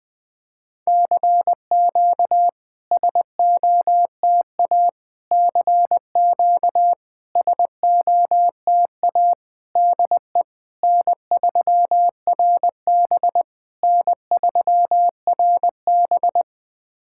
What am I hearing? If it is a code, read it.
Morse code, heard as a keyed tone, about 21 wpm, 690 Hz. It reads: CQ SOTA CQ SOTA DE N3RB N3RB
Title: SOTA QSO Practice 20wpm